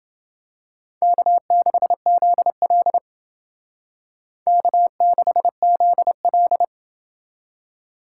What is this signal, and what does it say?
Morse code, heard as a keyed tone, about 29 wpm, 700 Hz. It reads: K6ZL K6ZL